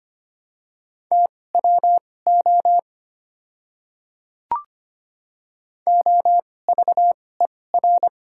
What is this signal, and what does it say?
Morse code, heard as a keyed tone, about 25 wpm, spaced as for 18 wpm, 700 Hz. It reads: TWO OVER
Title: 2x Character Spacing - Top 200 Words - 25wpm